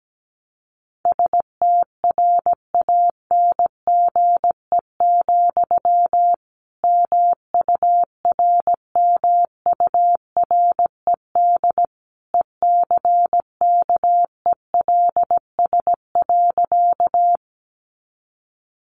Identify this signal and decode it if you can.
Morse code, heard as a keyed tone, about 17 wpm, 700 Hz. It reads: STRANGE, MURMURED ECKELS.
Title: A Sound Of Thunder 17wpm